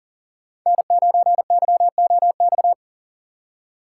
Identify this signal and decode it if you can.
Morse code, heard as a keyed tone, about 40 wpm, 700 Hz. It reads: N9YOX
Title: USA Call Signs - 5 Characters - 6x Character Spacing - 40wpm